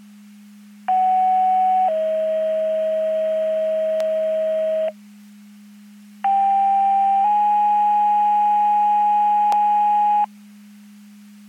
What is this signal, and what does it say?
Two-tone page, heard as a one-second tone, then a three-second tone, about 750 then 640 Hz, about 790 then 830 Hz. It is heard from 0.9 to 10.2 s.
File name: Portage 154.385 Paging